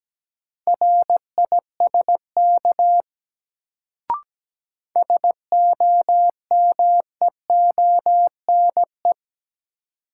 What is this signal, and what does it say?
Morse code, heard as a keyed tone, about 17 wpm, 700 Hz. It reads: RISK SOMEONE
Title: Sentences from Top 600 Words - New Word Drill - 17wpm